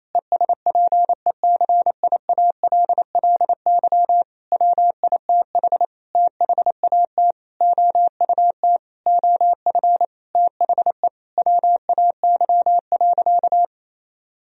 Morse code, heard as a keyed tone, about 28 wpm, 700 Hz. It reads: ESPECIALLY WITH THAT OUT OF THE WAY.
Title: Sentences from Top 1000 Words - Review Part 4 - 28wpm